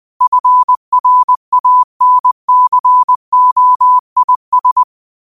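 Morse code, heard as a keyed tone, about 20 wpm, 1000 Hz. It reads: FRANCOIS
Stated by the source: Morse code practice at 20 wpm: famous quotations sent in telegraphy, from URVAG.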